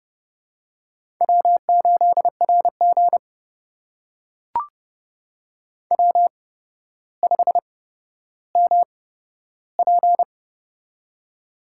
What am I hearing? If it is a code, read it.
Morse code, heard as a keyed tone, about 30 wpm, spaced as for 13 wpm, 700 Hz. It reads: W8RG W5MP
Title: USA Call Signs - 4 Characters - 8x Character Spacing - 30wpm